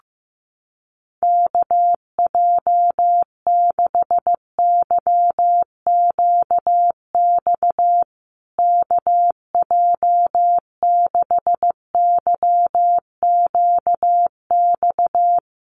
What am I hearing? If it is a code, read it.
Morse code, heard as a keyed tone, about 15 wpm, 700 Hz. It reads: KJ6YQX KJ6YQX
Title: USA Call Signs Repeated 3x 15wpm